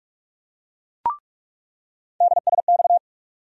Morse code, heard as a keyed tone, about 46 wpm, 700 Hz. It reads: DSX
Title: USA Call Sign Postfixes - 3 Letters 45wpm